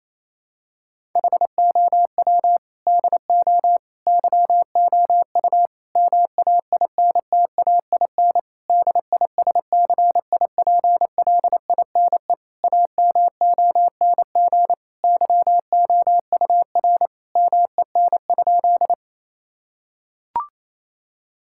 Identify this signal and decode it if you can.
Morse code, heard as a keyed tone, about 28 wpm, 700 Hz. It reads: HOW DO YOU MAINTAIN DISCIPLINE AMONG YOUR MEN?
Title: Sentences from Top 1000 Words - Review Part 6 - 28wpm